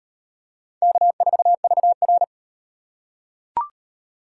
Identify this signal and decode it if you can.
Morse code, heard as a keyed tone, about 34 wpm, 700 Hz. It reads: K4VR